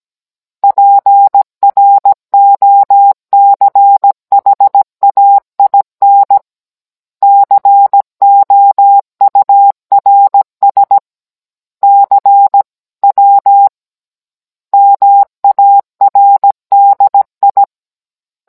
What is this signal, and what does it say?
Morse code, heard as a keyed tone, about 17 wpm, 800 Hz. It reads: PROCHAIN COURS CW MARDI